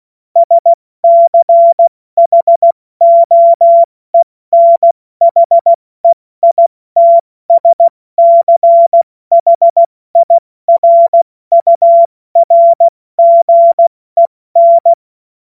Morse code, heard as a keyed tone, about 16 wpm, 680 Hz. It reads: SCHOENHEITSCHIRURGEN
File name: Morsekurs / Witze / Witz 02 April 2024 70 F80